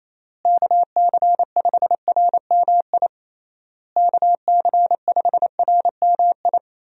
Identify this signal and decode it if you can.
Morse code, heard as a keyed tone, about 28 wpm, 700 Hz. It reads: KC5RMI KC5RMI
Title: USA Call Signs - 6 Characters - Repeated 3x - 28wpm